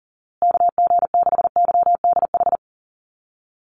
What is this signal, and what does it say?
Morse code, heard as a keyed tone, about 40 wpm, 700 Hz. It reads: KG6YDH
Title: USA Call Signs Repeated 3x 40wpm